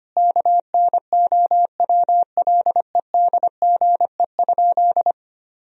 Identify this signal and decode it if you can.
Morse code, heard as a keyed tone, about 25 wpm, 700 Hz. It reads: KNOWLEDGE?
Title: Sentences from Top 1000 Words - Review Part 3 - 25wpm